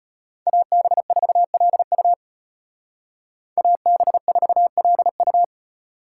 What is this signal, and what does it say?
Morse code, heard as a keyed tone, about 33 wpm, 700 Hz. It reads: AB4LU AB4LU